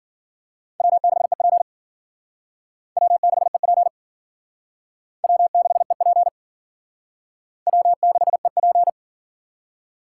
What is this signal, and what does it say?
Morse code, heard as a keyed tone, about 51 wpm, 700 Hz. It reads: W6EP W6EP W6EP W6EP